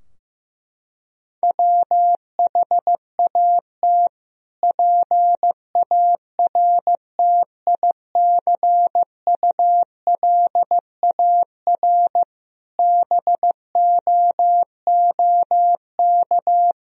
Morse code, heard as a keyed tone, about 15 wpm, 700 Hz. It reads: WHAT PARTICULAR BOOK
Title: Sentences from Top 1000 Words - Review Part 4 - 15wpm